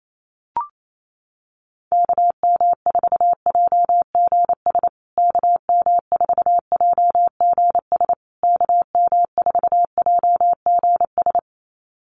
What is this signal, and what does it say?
Morse code, heard as a keyed tone, about 28 wpm, 700 Hz. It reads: KM4JGS KM4JGS KM4JGS
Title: USA Call Signs Repeated 3x 28wpm